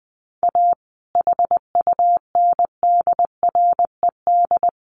Morse code, heard as a keyed tone, about 19 wpm, 700 Hz. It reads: A HUNDRED
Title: A Sound Of Thunder 20wpm